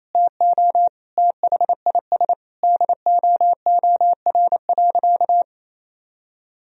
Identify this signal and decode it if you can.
Morse code, heard as a keyed tone, about 28 wpm, 700 Hz. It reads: TO THIS DOOR.
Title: Sentences from Top 1000 Words - Review Part 4 - 28wpm